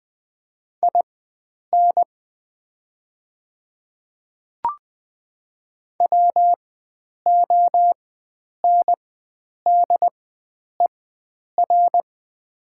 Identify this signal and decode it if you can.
Morse code, heard as a keyed tone, about 20 wpm, spaced as for 9 wpm, 690 Hz. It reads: IN WONDER
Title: 4x Character Spacing - Top 500 Words - 20wpm